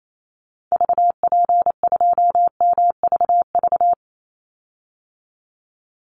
Morse code, heard as a keyed tone, about 28 wpm, 700 Hz. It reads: VP2MVV
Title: Call Signs - 28wpm